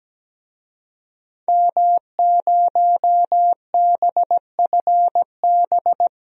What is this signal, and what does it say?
Morse code, heard as a keyed tone, about 17 wpm, 700 Hz. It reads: M0BFB
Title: DXCC-100 Call Signs - 5 Characters - 17wpm